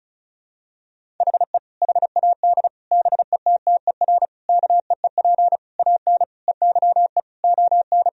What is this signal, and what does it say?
Morse code, heard as a keyed tone, about 36 wpm, 700 Hz. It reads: HE HAD BETTER KEEP AN EYE ON
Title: Sentences from Top 1000 Words - Review Part 1 - 35wpm